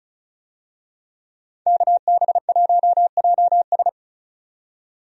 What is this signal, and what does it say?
Morse code, heard as a keyed tone, about 36 wpm, 700 Hz. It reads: KB1JS
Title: USA Call Signs - 5 Characters - 35wpm